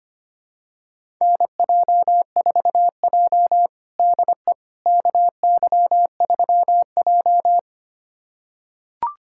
Morse code, heard as a keyed tone, about 25 wpm, 700 Hz. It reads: NJ4J DE KY3J
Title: USA Callsign DE Callsign - 4 Characters - 25wpm